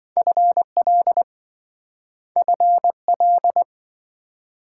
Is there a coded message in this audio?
Yes: Morse code from 0.2 to 3.6 s, heard as a keyed tone, about 28 wpm, 700 Hz. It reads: FL FL